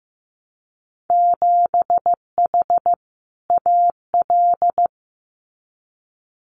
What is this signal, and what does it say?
Morse code, heard as a keyed tone, about 15 wpm, 700 Hz. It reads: 7H AL
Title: Winter Field Day Exchanges Encore 15wpm